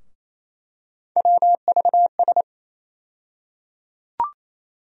Morse code, heard as a keyed tone, about 28 wpm, 700 Hz. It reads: WVS
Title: USA Call Sign Postfixes - 3 Letters 28wpm